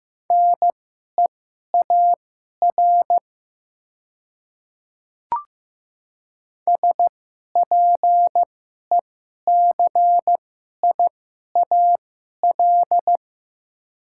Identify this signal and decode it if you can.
Morse code, heard as a keyed tone, about 15 wpm, spaced as for 11 wpm, 700 Hz. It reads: NEAR SPECIAL
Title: 2x Character Spacing - Top 500 Words - 15wpm